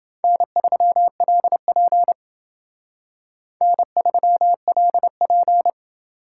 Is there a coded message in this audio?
Yes: Morse code from 0.2 to 5.7 s, heard as a keyed tone, about 29 wpm, 700 Hz. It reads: N3LP N3LP